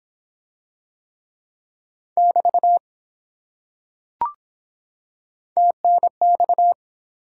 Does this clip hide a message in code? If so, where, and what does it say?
Yes: Morse code from 2.2 to 6.7 s, heard as a keyed tone, about 23 wpm, 700 Hz. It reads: <BT> TNX